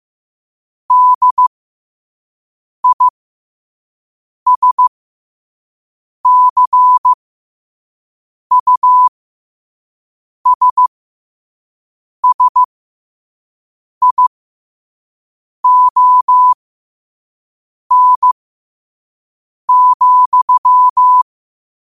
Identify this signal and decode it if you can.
Morse code, heard as a keyed tone, about 15 wpm, spaced as for 5 wpm, 1000 Hz. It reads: DISCUSSION,
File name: qotdfiles / 5wpm